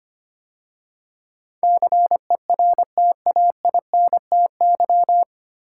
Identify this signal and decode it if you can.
Morse code, heard as a keyed tone, about 25 wpm, 700 Hz. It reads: CERTAINTY
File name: practice / 25wpm / 9 Letter Words - First 3 Letters Given - 25wpm